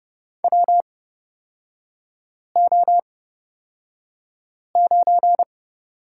Morse code, heard as a keyed tone, about 30 wpm, spaced as for 5 wpm, 700 Hz. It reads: WO9
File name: Lesson 37e - Letter x - Three Characters 30wpm